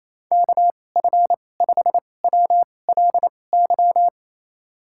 Morse code, heard as a keyed tone, about 27 wpm, spaced as for 20 wpm, 700 Hz. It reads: KF5WLY
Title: USA Call Signs - 6 Characters - 2x Character Spacing - 28wpm